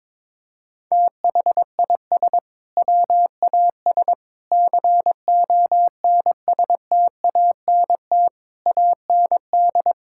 Morse code, heard as a keyed tone, about 22 wpm, 700 Hz. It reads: THIS WAS CONSTANT AND
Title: Sentences from Top 1600 Words 22wpm